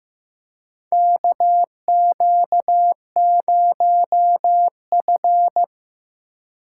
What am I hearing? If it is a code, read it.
Morse code, heard as a keyed tone, about 15 wpm, 700 Hz. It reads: KQ0F
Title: USA Call Signs - 4 Characters - 2x Character Spacing - 15wpm